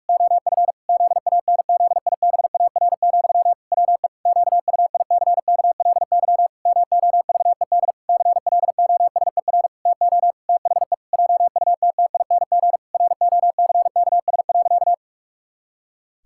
Morse code, heard as a keyed tone, about 46 wpm, 700 Hz. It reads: OF ZANZIBAR, WE QUICKLY MOVED CLOSER TO THE JUTTING ROCKS.
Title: Pangram Challenge v2 - 45wpm